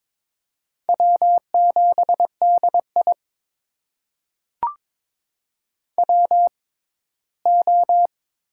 Morse code, heard as a keyed tone, about 22 wpm, spaced as for 15 wpm, 700 Hz. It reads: W7DI WO
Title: USA Call Signs - 4 Characters - 6x Character Spacing - 22wpm